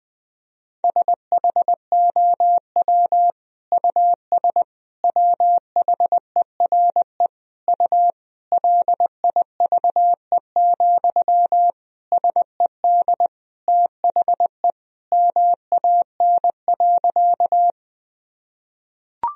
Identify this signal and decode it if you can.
Morse code, heard as a keyed tone, about 21 wpm, 700 Hz. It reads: SHOW US WHERE U LIVE, SED THE MAN.
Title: Sentences from Top 200 Words and Abbreviations 20wpm